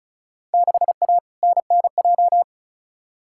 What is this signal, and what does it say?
Morse code, heard as a keyed tone, about 34 wpm, 700 Hz. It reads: 6A NNJ
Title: Field Day Exchanges Encore - 35wpm